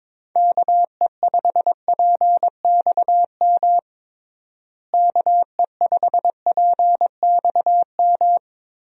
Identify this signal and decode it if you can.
Morse code, heard as a keyed tone, about 22 wpm, 700 Hz. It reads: KE5PXM KE5PXM
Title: USA Call Signs - 6 Characters - Repeated 3x - 22wpm